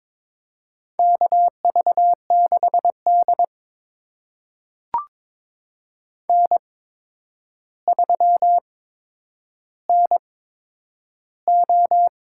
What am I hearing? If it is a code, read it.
Morse code, heard as a keyed tone, about 22 wpm, spaced as for 9 wpm, 700 Hz. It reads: KV6D N3NO
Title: USA Call Signs - 4 Characters - 8x Character Spacing - 22wpm